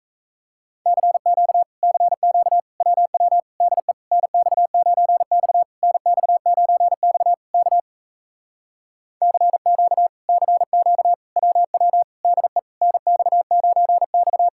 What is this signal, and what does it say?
Morse code, heard as a keyed tone, about 38 wpm, 700 Hz. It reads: CQ CQ WW DE NX9X NX9X K CQ CQ WW DE NX9X